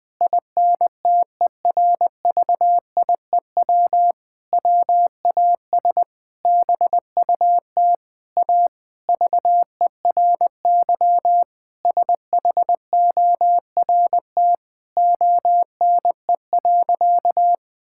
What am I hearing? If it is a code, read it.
Morse code, heard as a keyed tone, about 20 wpm, 700 Hz. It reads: INTERVIEW WAS BUT A VERY SHORT ONE.
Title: Sentences from Top 1600 Words 20wpm